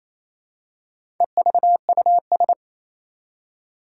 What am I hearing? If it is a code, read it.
Morse code, heard as a keyed tone, about 28 wpm, 700 Hz. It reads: EVUS
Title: Four Challenging Letter-Number Combinations 28wpm